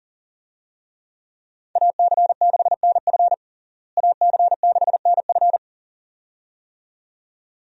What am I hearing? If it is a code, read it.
Morse code, heard as a keyed tone, about 40 wpm, 700 Hz. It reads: AC6NF AC6NF
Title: USA Call Signs - 5 Characters - Repeated 2x - 40wpm